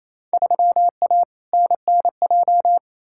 Morse code, heard as a keyed tone, about 28 wpm, 700 Hz. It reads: 3A NNJ
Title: Field Day Exchanges Encore - 28wpm